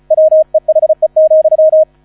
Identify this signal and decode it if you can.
Morse code, heard as a keyed tone, about 34 wpm, 630 Hz. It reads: WEHE,